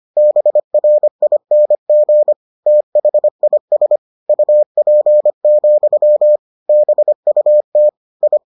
Morse code, heard as a keyed tone, about 25 wpm, 600 Hz. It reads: BRING THIS UP, BUT I